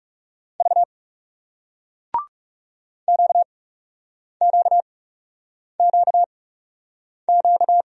Morse code, heard as a keyed tone, about 40 wpm, spaced as for 8 wpm, 700 Hz. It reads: V QQQQ